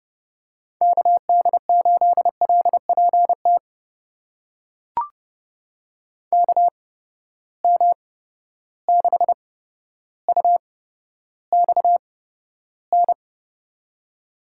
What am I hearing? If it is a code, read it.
Morse code, heard as a keyed tone, about 30 wpm, spaced as for 13 wpm, 700 Hz. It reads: KD8LPT KM6UXN
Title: USA Call Signs - 6 Characters - 8x Character Spacing - 30wpm